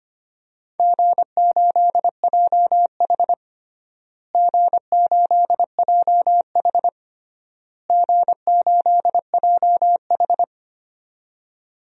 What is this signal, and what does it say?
Morse code, heard as a keyed tone, about 25 wpm, 700 Hz. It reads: G8JH G8JH G8JH
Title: DXCC-100 Call Signs - 4 Characters - Repeated 3x - 25wpm